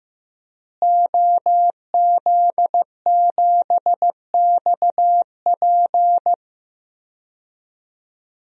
Morse code, heard as a keyed tone, about 15 wpm, 700 Hz. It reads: OZ7XP
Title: DXCC-100 Call Signs - 5 Characters - Repeated 2x - 15wpm